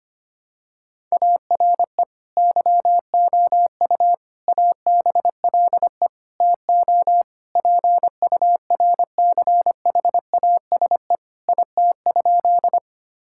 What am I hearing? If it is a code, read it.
Morse code, heard as a keyed tone, about 25 wpm, 700 Hz. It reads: ARE YOU ABLE TO PURCHASE IT?
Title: Sentences from Top 1000 Words - Review Part 5 - 25wpm